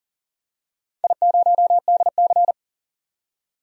Morse code, heard as a keyed tone, about 40 wpm, 700 Hz. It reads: I0DC
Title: DXCC-100 Call Signs - 4 Characters - 40wpm